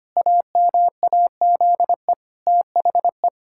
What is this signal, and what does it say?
Morse code, heard as a keyed tone, about 25 wpm, 700 Hz. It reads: AMAZE THE
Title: Pangram Challenge v2 - 25wpm